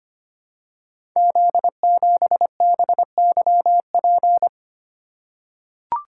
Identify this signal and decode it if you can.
Morse code, heard as a keyed tone, about 25 wpm, 700 Hz. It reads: Z7BYP
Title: Code Groups - 5 Letter:Numbers - 25wpm